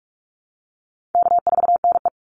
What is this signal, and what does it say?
Morse code, heard as a keyed tone, about 44 wpm, 700 Hz. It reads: K4NE
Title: Easy Call Signs Repeated 3x - 45wpm